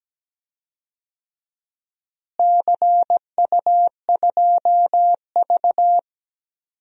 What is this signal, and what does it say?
Morse code, heard as a keyed tone, about 17 wpm, 700 Hz. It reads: CU2V
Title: DXCC-100 Call Signs - 4 Characters - Repeated 2x - 17wpm